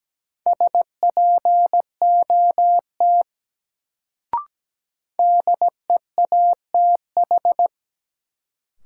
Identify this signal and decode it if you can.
Morse code, heard as a keyed tone, about 17 wpm, 700 Hz. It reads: SPOT DEATH
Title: Sentences from Top 1100 Words - New Word Drill Alt - 17wpm